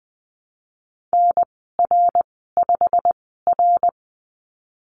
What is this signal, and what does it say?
Morse code, heard as a keyed tone, about 20 wpm, spaced as for 15 wpm, 700 Hz. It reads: NR5R
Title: 128 Lesson 21g - Number 6 - Call Signs 20wpm